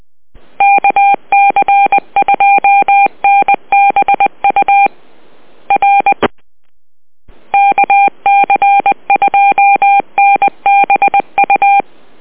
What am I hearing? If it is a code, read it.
Morse code, heard as a keyed tone, about 20 wpm, 800 Hz. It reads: KC2NBU R KC2NBU